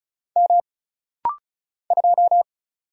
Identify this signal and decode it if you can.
Morse code, heard as a keyed tone, about 34 wpm, 700 Hz. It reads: M 2
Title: Single Letter-Number Alt - Rapid-Fire 35wpm